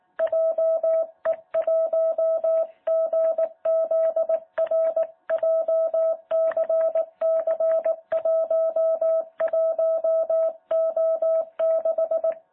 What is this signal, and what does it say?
Morse code, heard as a keyed tone, about 18 wpm, 650 Hz. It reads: JE1GZR JCC11O6
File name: jh8cbh / 2021cw